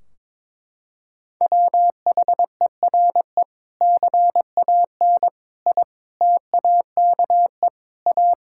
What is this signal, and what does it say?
Morse code, heard as a keyed tone, about 22 wpm, 700 Hz. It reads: WHERE CAN I TAKE A
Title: Sentences from Top 1600 Words 22wpm